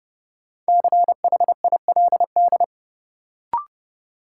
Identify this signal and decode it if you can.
Morse code, heard as a keyed tone, about 29 wpm, 700 Hz. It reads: CHILD
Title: Sets of 5 Words Encore - New Word Drill - 30wpm